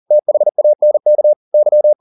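Morse code, heard as a keyed tone, about 40 wpm, 600 Hz. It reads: THANK Y
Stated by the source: Morse code at 40 wpm, practice text, CW med Gnister.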